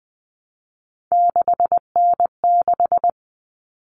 Morse code, heard as a keyed tone, about 20 wpm, 700 Hz. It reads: 6N6
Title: All Cut Numbers - 3-Digit 20wpm